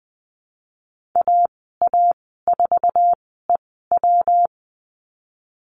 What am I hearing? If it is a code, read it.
Morse code, heard as a keyed tone, about 20 wpm, spaced as for 15 wpm, 700 Hz. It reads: AA4EW
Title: 140 Lesson 23g - Letter f - Call Signs 20wpm